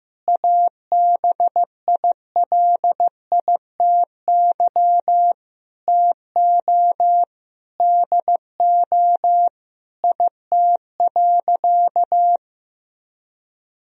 Morse code, heard as a keyed tone, about 16 wpm, 700 Hz. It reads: ABILITY TO DO IT.
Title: Sentences from Top 1000 Words - Review Part 3 - 15wpm